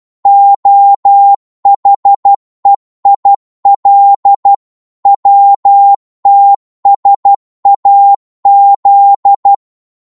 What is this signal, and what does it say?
Morse code, heard as a keyed tone, about 12 wpm, 800 Hz. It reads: OHEIL WTSAZ